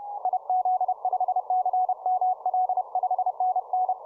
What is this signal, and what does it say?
Morse code, heard as a keyed tone, about 30 wpm, 740 Hz. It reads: IZ5CML 5NN
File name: simple-file-list / k5d 20m cw